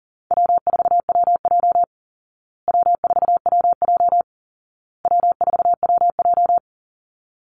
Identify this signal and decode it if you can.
Morse code, heard as a keyed tone, about 40 wpm, 700 Hz. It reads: W4WJ W4WJ W4WJ
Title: Easy Call Signs Repeated 3x - 40wpm